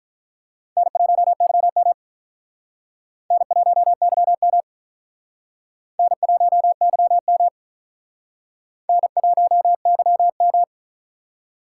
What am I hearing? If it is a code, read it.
Morse code, heard as a keyed tone, about 44 wpm, 700 Hz. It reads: N1YM N1YM N1YM N1YM
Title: Speed Racing - USA 1x2 Call Signs 35wpm